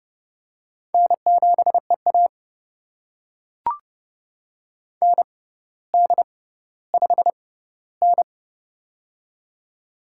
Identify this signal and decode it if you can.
Morse code, heard as a keyed tone, about 28 wpm, spaced as for 15 wpm, 700 Hz. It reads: N7EA ND5N
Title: USA Call Signs - 4 Characters - 6x Character Spacing - 30wpm